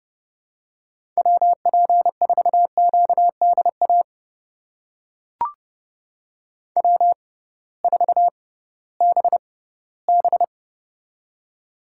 Morse code, heard as a keyed tone, about 32 wpm, spaced as for 18 wpm, 700 Hz. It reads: WP4QDA W4BB
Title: USA Call Signs - 6x Character Spacing - 30wpm